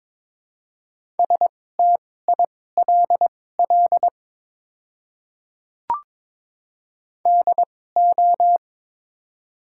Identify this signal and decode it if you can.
Morse code, heard as a keyed tone, about 22 wpm, spaced as for 16 wpm, 700 Hz. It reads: STILL DO
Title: 2x Character Spacing - Top 300 Words - 22wpm